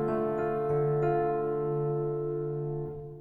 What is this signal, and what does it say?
Dial tone (North America), 350 + 440 Hz, with noise.